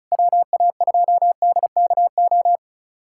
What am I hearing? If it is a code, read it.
Morse code, heard as a keyed tone, about 34 wpm, 690 Hz. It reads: WA2DKO
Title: USA Callsign DE Callsign - 6 Characters - 35wpm